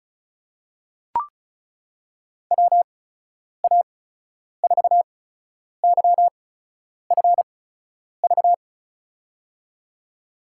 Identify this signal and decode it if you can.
Morse code, heard as a keyed tone, about 36 wpm, spaced as for 10 wpm, 700 Hz. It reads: WA4YFV
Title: USA Call Signs - 8x Character Spacing - 35wpm